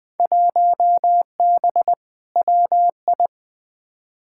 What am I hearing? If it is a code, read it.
Morse code, heard as a keyed tone, about 20 wpm, 700 Hz. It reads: 1B WI
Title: Field Day Exchanges - 20wpm